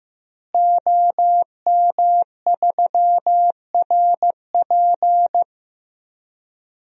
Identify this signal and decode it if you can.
Morse code, heard as a keyed tone, about 15 wpm, 700 Hz. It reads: OM3RP
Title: DXCC-100 Call Signs - 5 Characters - Repeated 3x - 15wpm